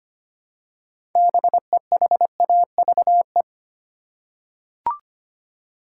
Morse code, heard as a keyed tone, about 25 wpm, 700 Hz. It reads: BEHAVE
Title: Six Letter Words - 25wpm